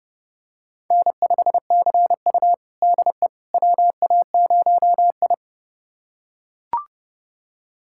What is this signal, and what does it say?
Morse code, heard as a keyed tone, about 32 wpm, 700 Hz. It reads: N5CU DE WA0I
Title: USA Callsign DE Callsign - 4 Characters - 30wpm